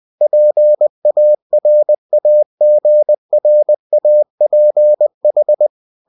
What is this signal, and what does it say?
Morse code, heard as a keyed tone, about 20 wpm, 600 Hz. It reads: PARAGRAPH